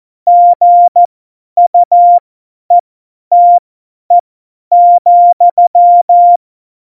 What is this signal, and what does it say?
Morse code, heard as a keyed tone, about 14 wpm, spaced as for 10 wpm, 700 Hz. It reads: GUETE,
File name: Morsekurs / Witze / Witz 03.Dezember 2019, 50 F70